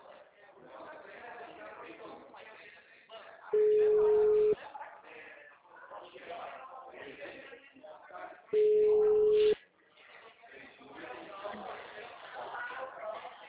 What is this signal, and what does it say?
Ringback tone (Europe, 425 Hz), with speech.
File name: out- -2015- - - .11827